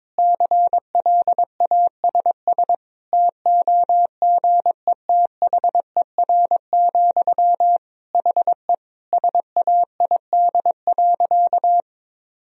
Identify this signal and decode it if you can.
Morse code, heard as a keyed tone, about 22 wpm, 700 Hz. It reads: CLASS TOGETHER, HE SAID.
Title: Sentences from Top 1000 Words - Review Part 2 - 22wpm